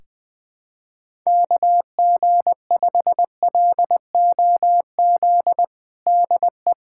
Morse code, heard as a keyed tone, about 20 wpm, 700 Hz. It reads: KG5LOZ DE
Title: USA Callsign DE Callsign - 6 Characters - 20wpm